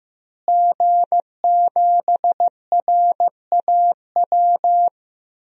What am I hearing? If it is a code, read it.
Morse code, heard as a keyed tone, about 16 wpm, 700 Hz. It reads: G7RAW
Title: DXCC-100 Call Signs - 5 Characters - Repeated 3x - 15wpm